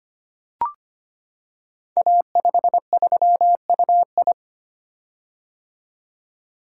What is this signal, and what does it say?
Morse code, heard as a keyed tone, about 25 wpm, 700 Hz. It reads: A53UI
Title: Code Groups - 5 Letter:Numbers - 25wpm